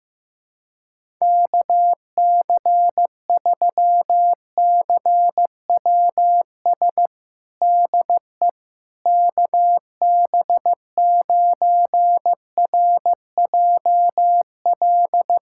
Morse code, heard as a keyed tone, about 15 wpm, 700 Hz. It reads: KC3CWS DE KB9RJL
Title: USA Callsign DE Callsign - 6 Characters - 15wpm